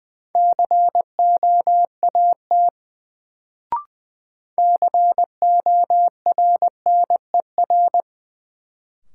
Morse code, heard as a keyed tone, about 20 wpm, 700 Hz. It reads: COAT CORNER
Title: Sentences from Top 1600 Words - New Word Drill - 20wpm